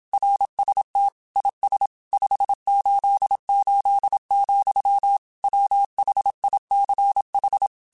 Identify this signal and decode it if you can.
Morse code, heard as a keyed tone, about 27 wpm, 800 Hz. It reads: RST IS 588, WHICH